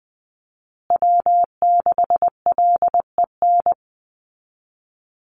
Morse code, heard as a keyed tone, about 19 wpm, 700 Hz. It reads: W6LEN
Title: Call Signs Repeated 3x - 20wpm